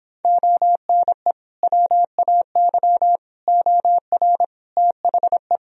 Morse code, heard as a keyed tone, about 26 wpm, 700 Hz. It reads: ONE WAY OR THE
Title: Speed-Racing - Sets of 5 Words 22wpm